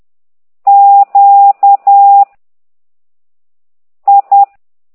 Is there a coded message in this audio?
Yes: Morse code from 0.7 to 4.4 s, heard as a keyed tone, about 10 wpm, 800 Hz. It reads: Q I